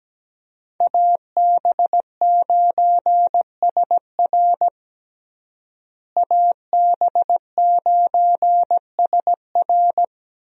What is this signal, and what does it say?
Morse code, heard as a keyed tone, about 17 wpm, 700 Hz. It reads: AB9SR AB9SR
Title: USA Call Signs - 5 Characters - Repeated 2x - 17wpm